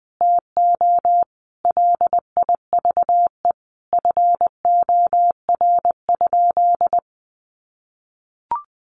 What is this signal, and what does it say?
Morse code, heard as a keyed tone, about 20 wpm, 700 Hz. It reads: TO LIVE FOR?
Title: Sentences from Top 200 Words 20wpm